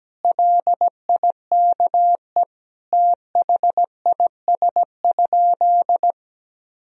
Morse code, heard as a keyed tone, about 17 wpm, 700 Hz. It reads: LIKE THIS?
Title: Sentences from Top 200 Words and Abbreviations 17wpm